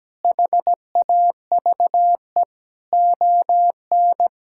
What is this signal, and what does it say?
Morse code, heard as a keyed tone, about 17 wpm, 700 Hz. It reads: HAVE ON
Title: Sentences from Top 1000 Words - Review Part 6 - 17wpm